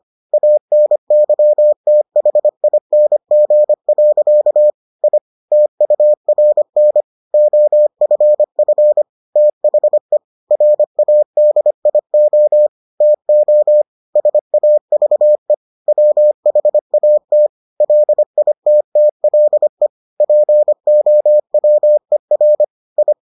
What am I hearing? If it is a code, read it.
Morse code, heard as a keyed tone, about 25 wpm, 600 Hz. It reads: ANYTHING. I TURN OFF THE RADIO TO SAVE WHAT LITTLE POWER I